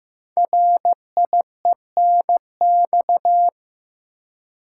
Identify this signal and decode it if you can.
Morse code, heard as a keyed tone, about 15 wpm, 700 Hz. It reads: RIENX
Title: Code Groups - 5 Letters - 15wpm